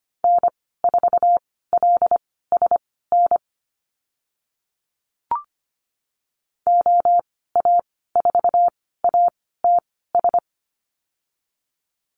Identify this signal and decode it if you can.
Morse code, heard as a keyed tone, about 25 wpm, spaced as for 16 wpm, 700 Hz. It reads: N4LSN OA4ATS
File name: Lesson 15g - Number 5 - Call Signs 25wpm